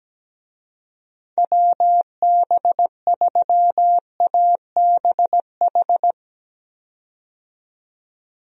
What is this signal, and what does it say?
Morse code, heard as a keyed tone, about 17 wpm, 700 Hz. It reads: WB3ABH
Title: USA Call Signs - 6 Characters - 17wpm